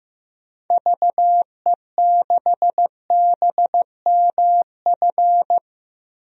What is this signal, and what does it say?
Morse code, heard as a keyed tone, about 15 wpm, 700 Hz. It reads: VE6BMF
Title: DXCC-100 Call Signs - 6 Characters - 15wpm